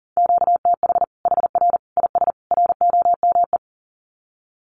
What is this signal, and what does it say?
Morse code, heard as a keyed tone, about 40 wpm, 700 Hz. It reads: QTH HR IS ROME
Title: Common QSO Phrases 40wpm